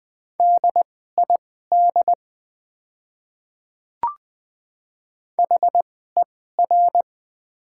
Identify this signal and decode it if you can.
Morse code, heard as a keyed tone, about 20 wpm, spaced as for 15 wpm, 700 Hz. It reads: DID HER
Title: 2x Character Spacing - Top 100 Words - 20wpm